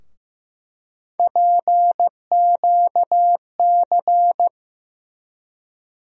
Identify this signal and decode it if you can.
Morse code, heard as a keyed tone, about 15 wpm, 700 Hz. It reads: PQC
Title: USA Call Sign Postfixes - 3 Letters 15wpm